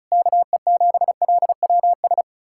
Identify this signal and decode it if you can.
Morse code, heard as a keyed tone, about 46 wpm, 700 Hz. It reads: KE7LWS